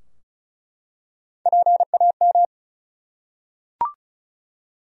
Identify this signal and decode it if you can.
Morse code, heard as a keyed tone, about 36 wpm, 700 Hz. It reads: PAM
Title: USA Common Names - Max 4 Letters - 35wpm